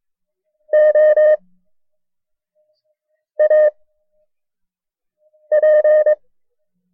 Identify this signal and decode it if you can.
Morse code, heard as a keyed tone, about 22 wpm, 610 Hz. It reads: O A P
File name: FreeFiles / CW-practice / KB6MT-tape1-side2